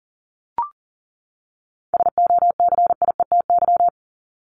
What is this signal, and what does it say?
Morse code, heard as a keyed tone, about 40 wpm, 700 Hz. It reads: SOCIETY
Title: Top 501-750 Words 40wpm